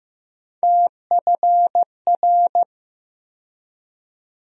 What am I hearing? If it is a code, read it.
Morse code, heard as a keyed tone, about 15 wpm, 700 Hz. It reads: TFR
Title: USA Call Sign Postfixes - 3 Letters 15wpm